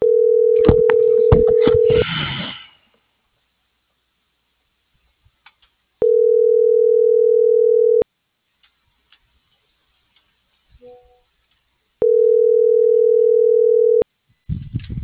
Ringback tone (North America), 440 + 480 Hz.